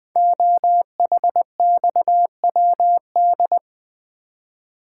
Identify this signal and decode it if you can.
Morse code, heard as a keyed tone, about 20 wpm, 700 Hz. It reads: OHXWD
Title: Code Groups - 5 Letters - 20wpm